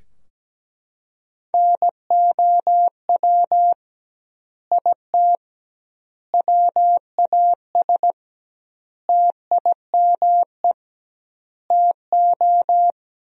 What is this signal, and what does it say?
Morse code, heard as a keyed tone, about 17 wpm, 700 Hz. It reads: NOW IT WAS TIME TO
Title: 2x Word-Spacing - Sentences from Top 200 Words - 17wpm